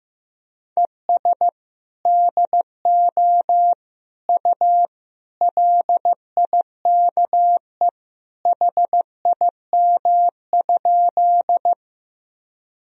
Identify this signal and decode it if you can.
Morse code, heard as a keyed tone, about 15 wpm, 700 Hz. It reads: ES DO U LIKE HIM?
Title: Sentences from Top 100 Words and Abbreviations 15wpm